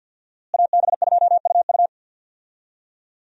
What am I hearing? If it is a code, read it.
Morse code, heard as a keyed tone, about 52 wpm, 690 Hz. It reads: AB2UU
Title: USA Call Signs - 5 Characters - Repeated 3x - 50wpm